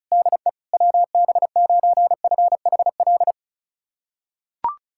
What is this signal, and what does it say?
Morse code, heard as a keyed tone, about 36 wpm, 690 Hz. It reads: DE WB9FHL
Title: USA Callsign DE Callsign - 6 Characters - 35wpm